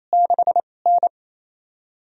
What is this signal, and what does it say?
Morse code, heard as a keyed tone, about 28 wpm, 700 Hz. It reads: 6 N
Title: USA Call Signs - 4 Characters - 2x Character Spacing - 28wpm